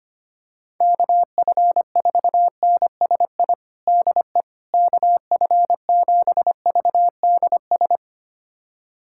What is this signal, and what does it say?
Morse code, heard as a keyed tone, about 25 wpm, 700 Hz. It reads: KF4NSI DE KF7VDS
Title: USA Callsign DE Callsign - 6 Characters - 25wpm